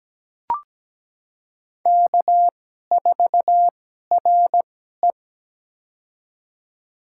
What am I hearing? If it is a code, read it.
Morse code, heard as a keyed tone, about 17 wpm, spaced as for 12 wpm, 700 Hz. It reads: K4RE
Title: USA Call Signs - 2x Character Spacing - 17wpm